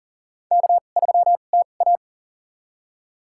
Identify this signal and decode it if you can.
Morse code, heard as a keyed tone, about 40 wpm, spaced as for 29 wpm, 700 Hz. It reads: K3TA
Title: USA Call Signs - 4 Characters - 2x Character Spacing - 40wpm